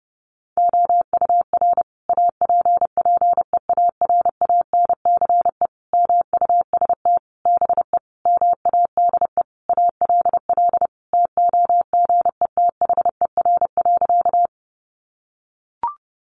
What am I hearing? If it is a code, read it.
Morse code, heard as a keyed tone, about 30 wpm, 700 Hz. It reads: OUR APPEARANCE MUST BE MADE ALL TOGETHER.
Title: Sentences from Top 1100 Words 30wpm